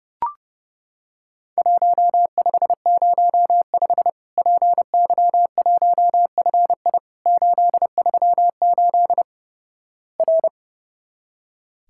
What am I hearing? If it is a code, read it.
Morse code, heard as a keyed tone, about 30 wpm, 700 Hz. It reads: 1505 PY1FI 838 R
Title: Worked All Europe DX Contest - QTC Exchange - 30wpm